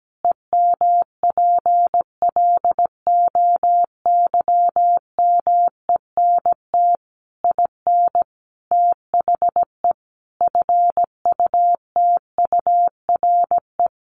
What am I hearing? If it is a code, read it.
Morse code, heard as a keyed tone, about 17 wpm, 700 Hz. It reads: EMPLOYMENT IN THE FUTURE
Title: Sentences from Top 1500 Words 17wpm